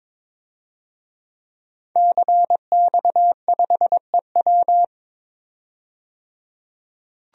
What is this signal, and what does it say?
Morse code, heard as a keyed tone, about 22 wpm, 700 Hz. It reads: CX5EW
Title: DXCC-100 Call Signs - 5 Characters - 22wpm